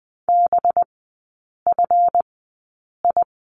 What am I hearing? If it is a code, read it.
Morse code, heard as a keyed tone, about 19 wpm, spaced as for 8 wpm, 700 Hz. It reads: BFI
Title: 184 Lesson 30e - Letter b - Three Characters 20wpm